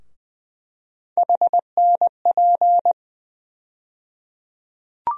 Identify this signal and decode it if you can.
Morse code, heard as a keyed tone, about 20 wpm, 700 Hz. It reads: HNP
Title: USA Call Sign Postfixes - 3 Letters 20wpm